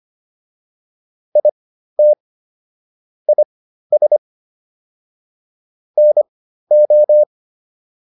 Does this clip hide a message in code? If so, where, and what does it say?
Yes: Morse code from 1.4 to 7.2 s, heard as a keyed tone, about 25 wpm, spaced as for 13 wpm, 600 Hz. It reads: IT IS NO